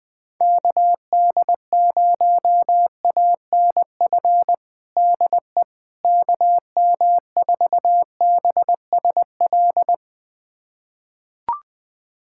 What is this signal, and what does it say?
Morse code, heard as a keyed tone, about 21 wpm, 700 Hz. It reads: KD0ANF DE KM4BSL
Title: USA Callsign DE Callsign - 6 Characters - 20wpm